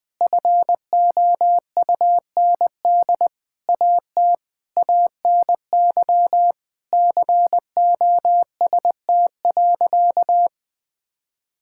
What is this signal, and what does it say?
Morse code, heard as a keyed tone, about 20 wpm, 700 Hz. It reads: FOUND AT ANY COST.
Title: Sentences from Top 1000 Words - Review Part 5 - 20wpm